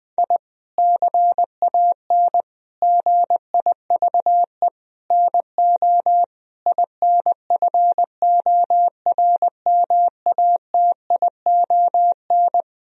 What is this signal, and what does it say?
Morse code, heard as a keyed tone, about 21 wpm, 700 Hz. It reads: I CAN GIVE NO INFORMATION
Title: Sentences from Top 1000 Words - Review Part 3 - 20wpm